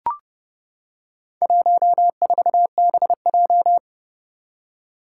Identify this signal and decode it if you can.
Morse code, heard as a keyed tone, about 30 wpm, 700 Hz. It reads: 14BJ
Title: Four Challenging Letter-Number Combinations 30wpm